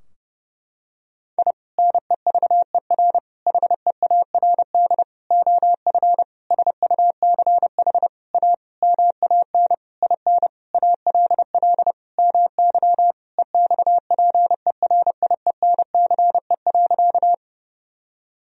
Morse code, heard as a keyed tone, about 32 wpm, 700 Hz. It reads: I NEVER HEARD OF SUCH A MAN IN ALL MY EXPERIENCE.
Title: Sentences from Top 1000 Words - Review Part 3 - 30wpm